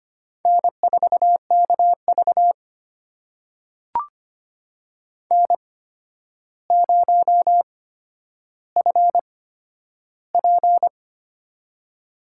Morse code, heard as a keyed tone, about 25 wpm, spaced as for 11 wpm, 700 Hz. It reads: N4KV N0FP
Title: USA Call Signs - 4 Characters - 8x Character Spacing - 25wpm